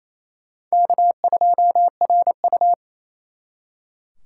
Morse code, heard as a keyed tone, about 35 wpm, 700 Hz. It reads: K2RU